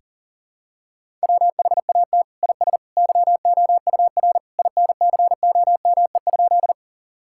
Morse code, heard as a keyed tone, about 37 wpm, 700 Hz. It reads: WHAT IS YOUR INCOME?
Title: Sentences from Top 1000 Words - Review Part 5 - 40wpm